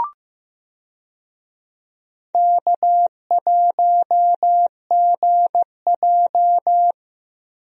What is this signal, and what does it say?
Morse code, heard as a keyed tone, about 15 wpm, 700 Hz. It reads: K1GJ
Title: USA Call Signs - 4 Characters - Repeated 2x - 15wpm